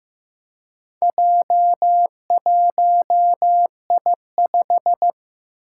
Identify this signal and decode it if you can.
Morse code, heard as a keyed tone, about 15 wpm, 700 Hz. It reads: J1I5
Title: Four Challenging Letter-Number Combinations 15wpm